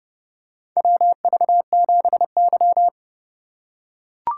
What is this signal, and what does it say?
Morse code, heard as a keyed tone, about 32 wpm, 700 Hz. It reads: WV7Y
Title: USA Call Signs - 4 Characters - 4x Character Spacing - 30wpm